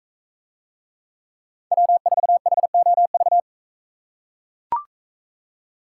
Morse code, heard as a keyed tone, about 42 wpm, 700 Hz. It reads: W4HOV